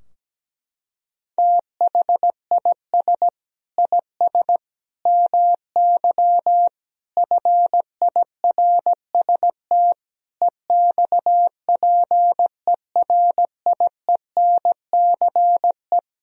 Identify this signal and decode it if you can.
Morse code, heard as a keyed tone, about 17 wpm, 700 Hz. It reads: THIS IS MY FIRST EXPERIENCE
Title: Sentences from Top 1000 Words - Review Part 3 - 17wpm